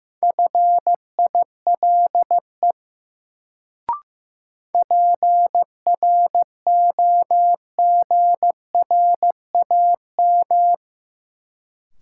Morse code, heard as a keyed tone, about 15 wpm, 700 Hz. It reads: FILE PROGRAM
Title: Sentences from Top 1000 Words - Review Part 4 - New Word Drill - 15wpm